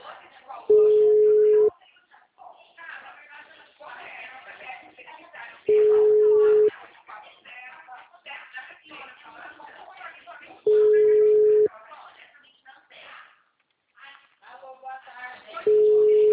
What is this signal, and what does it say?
Ringback tone (Europe, 425 Hz), with speech.
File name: out- -2006- - - .74133